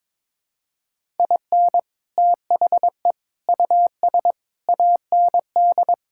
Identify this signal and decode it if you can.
Morse code, heard as a keyed tone, about 22 wpm, 700 Hz. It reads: IN THE US AND
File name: practice / 22wpm / Sets of 7 Words - 22wpm